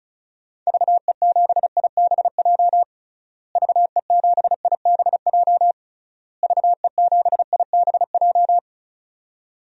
Morse code, heard as a keyed tone, about 34 wpm, 700 Hz. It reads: VE7IBJ VE7IBJ VE7IBJ
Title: DXCC-100 Call Signs - 6 Characters - Repeated 3x - 35wpm